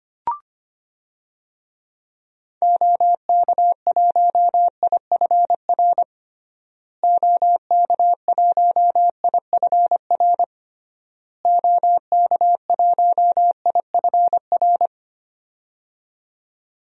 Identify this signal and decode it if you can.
Morse code, heard as a keyed tone, about 25 wpm, 700 Hz. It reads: OK1IFR OK1IFR OK1IFR
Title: DXCC-100 Call Signs - 6 Characters - Repeated 3x - 25wpm